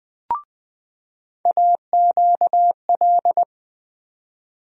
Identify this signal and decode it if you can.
Morse code, heard as a keyed tone, about 20 wpm, 700 Hz. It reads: AQL
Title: USA Call Sign Postfixes - 3 Letters 20wpm